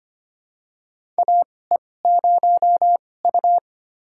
Morse code, heard as a keyed tone, about 25 wpm, spaced as for 18 wpm, 700 Hz. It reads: AE0U
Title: USA Call Signs - 4 Characters - 2x Character Spacing - 25wpm